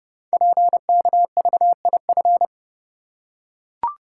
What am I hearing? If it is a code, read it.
Morse code, heard as a keyed tone, about 30 wpm, 700 Hz. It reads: PKVIF
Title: Code Groups - 5 Letters - 30wpm